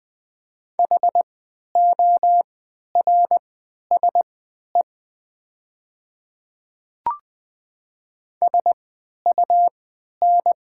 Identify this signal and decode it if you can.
Morse code, heard as a keyed tone, about 20 wpm, spaced as for 11 wpm, 700 Hz. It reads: HORSE SUN
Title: 3x Character Spacing - Top 300 Words - 20wpm